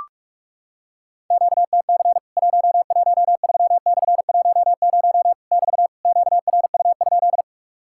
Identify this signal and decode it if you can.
Morse code, heard as a keyed tone, about 44 wpm, 700 Hz. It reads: QTC 113/10 <BT> QRU?
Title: Worked All Europe DX Contest - QTC Exchange - 45wpm